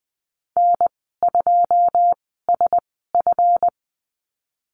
Morse code, heard as a keyed tone, about 19 wpm, spaced as for 14 wpm, 700 Hz. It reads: N2SF
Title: 186 Lesson 30g - Letter b - Call Signs 20wpm